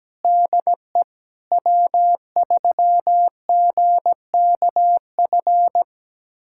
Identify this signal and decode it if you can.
Morse code, heard as a keyed tone, about 17 wpm, 700 Hz. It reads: DE W3GKF
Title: USA Callsign DE Callsign - 5 Characters - 17wpm